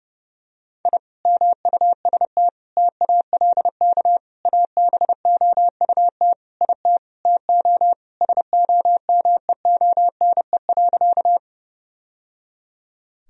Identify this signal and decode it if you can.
Morse code, heard as a keyed tone, about 32 wpm, 700 Hz. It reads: I MUST TALK ABOUT IT TO SOMEONE.
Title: Sentences from Top 1000 Words - Review Part 4 - 30wpm